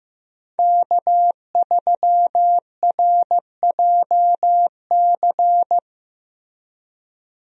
Morse code, heard as a keyed tone, about 15 wpm, 700 Hz. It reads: K3RJC
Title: USA Call Signs - 5 Characters - Repeated 2x - 15wpm